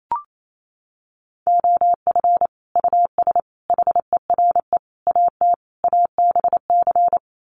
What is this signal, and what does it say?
Morse code, heard as a keyed tone, about 28 wpm, 700 Hz. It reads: OF US HERE AT ABC
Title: Sets of 5 Words 28wpm